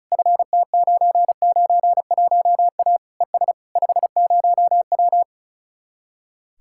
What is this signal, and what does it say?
Morse code, heard as a keyed tone, about 46 wpm, 700 Hz. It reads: FT991A ES 50W